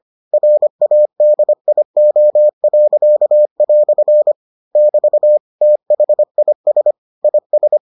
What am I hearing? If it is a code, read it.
Morse code, heard as a keyed tone, about 25 wpm, 600 Hz. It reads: RADIO." <BT> THIS IS